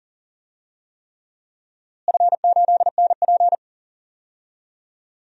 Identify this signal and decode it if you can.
Morse code, heard as a keyed tone, about 40 wpm, 700 Hz. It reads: F8NP
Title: DXCC-100 Call Signs - 4 Characters - Repeated 3x - 40wpm